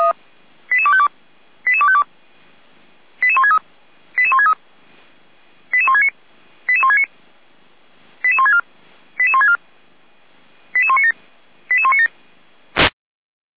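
Five-tone selective call (ZVEI), heard as quick runs of five tones, tones of about 70 ms. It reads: 88252 88252 88163 88163 88169 88169 88164 88164 88197 88197
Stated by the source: SDR-received ZVEI five-tone selective call, code 88255.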